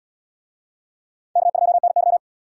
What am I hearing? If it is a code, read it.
Morse code, heard as a keyed tone, about 68 wpm, 700 Hz. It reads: K2NY